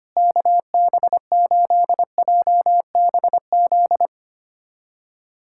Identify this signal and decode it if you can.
Morse code, heard as a keyed tone, about 25 wpm, 700 Hz. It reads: KB8JBZ
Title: USA Callsign DE Callsign 25wpm